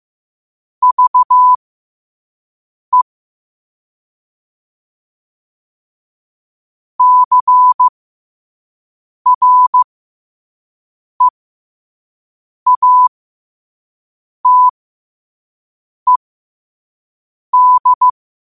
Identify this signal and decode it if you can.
Morse code, heard as a keyed tone, about 15 wpm, spaced as for 5 wpm, 1000 Hz. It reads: VE CREATED
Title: 5 WPM morse code quotes for Fri, 01 Mar 2024